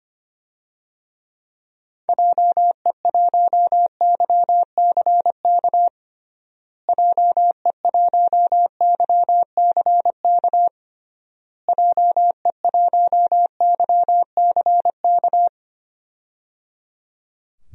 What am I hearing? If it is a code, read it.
Morse code, heard as a keyed tone, about 25 wpm, 700 Hz. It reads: JE1YCK JE1YCK JE1YCK
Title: DXCC-100 Call Signs - 6 Characters - Repeated 3x - 25wpm